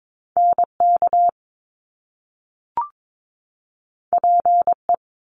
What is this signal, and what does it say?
Morse code, heard as a keyed tone, about 22 wpm, 700 Hz. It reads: NK PE
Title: Top Two Letter Combinations in Words 22wpm